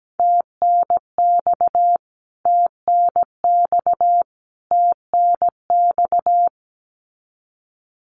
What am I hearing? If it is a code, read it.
Morse code, heard as a keyed tone, about 17 wpm, 700 Hz. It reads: TNX TNX TNX
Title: Abbreviations Repeated 3x - 17wpm